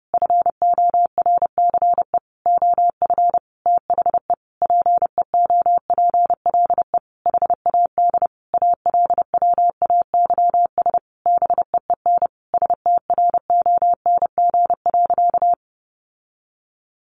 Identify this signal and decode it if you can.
Morse code, heard as a keyed tone, about 30 wpm, 700 Hz. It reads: FORCE OF THE PEOPLE HAD ALWAYS BEEN STRONG.
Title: Sentences from Top 1400 Words 30wpm